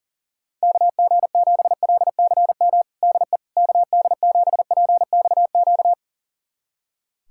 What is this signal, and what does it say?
Morse code, heard as a keyed tone, about 40 wpm, 700 Hz. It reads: KG7LCM DE KD7PXQ
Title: USA Callsign DE Callsign - 6 Characters - 40wpm